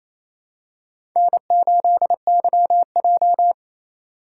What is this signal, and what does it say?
Morse code, heard as a keyed tone, about 28 wpm, 700 Hz. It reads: N8YJ
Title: USA Call Signs - 4x Character Spacing - 28wpm